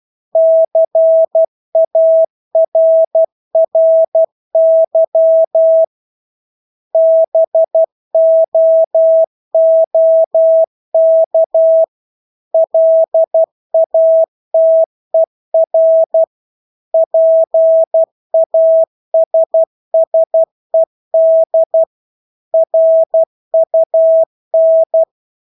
Morse code, heard as a keyed tone, about 12 wpm, 650 Hz. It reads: CARRY BOOK LATER PASSED RUN
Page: Engelske ord 12 wpm